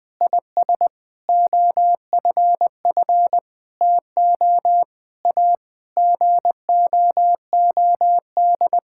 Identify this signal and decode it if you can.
Morse code, heard as a keyed tone, about 20 wpm, 700 Hz. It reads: IS OFF TO A GOOD
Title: Sentences from Top 1600 Words 20wpm